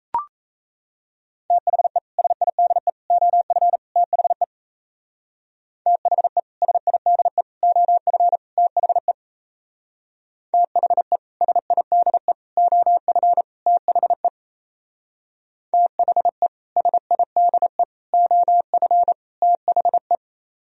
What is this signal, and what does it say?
Morse code, heard as a keyed tone, about 35 wpm, 700 Hz. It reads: THE SIDE OF THE THE SIDE OF THE THE SIDE OF THE THE SIDE OF THE
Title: Speed-Racing - Sets of 4 Words 28wpm